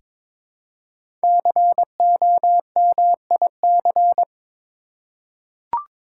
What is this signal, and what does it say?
Morse code, heard as a keyed tone, about 23 wpm, 700 Hz. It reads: COMIC
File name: 5 Letter Words - First 3 Letters Given - 22wpm